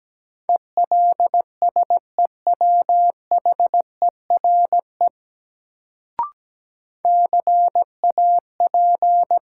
Morse code, heard as a keyed tone, about 17 wpm, 700 Hz. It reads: ELSEWHERE CAP
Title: Sentences from Top 1600 Words - New Word Drill Alt - 17wpm